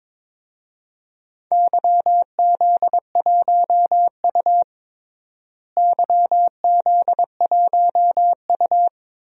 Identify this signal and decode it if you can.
Morse code, heard as a keyed tone, about 22 wpm, 700 Hz. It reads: YZ1U YZ1U
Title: DXCC-100 Call Signs - 4 Characters - Repeated 2x - 22wpm